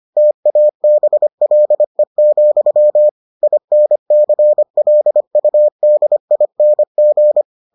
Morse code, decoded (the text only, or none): TABLE, INCLUDING